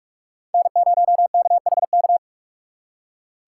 Morse code, heard as a keyed tone, about 59 wpm, spaced as for 52 wpm, 700 Hz. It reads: N0KHK
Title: Speed Racing - USA Call Signs 45wpm